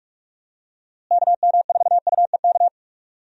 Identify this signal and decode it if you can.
Morse code, heard as a keyed tone, about 46 wpm, 700 Hz. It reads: KM4UEK
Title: USA Call Signs - 6 Characters - Repeated 3x - 45wpm